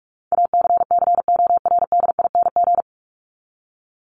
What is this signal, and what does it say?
Morse code, heard as a keyed tone, about 44 wpm, 700 Hz. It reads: ACCORDING
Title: LY Suffix - Top 100 Words with Base Words 45wpm